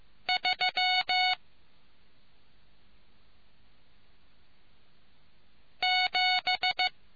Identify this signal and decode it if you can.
Morse code, heard as a keyed tone, about 15 wpm, 740 Hz. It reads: 3 7